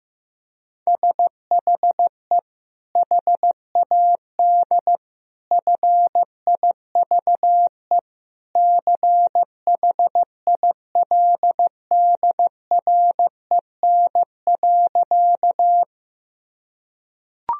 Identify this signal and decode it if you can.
Morse code, heard as a keyed tone, about 14 wpm, 700 Hz. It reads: SHE HAD FIVE CHILDREN.
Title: Sentences from Top 1000 Words - Review Part 2 - 15wpm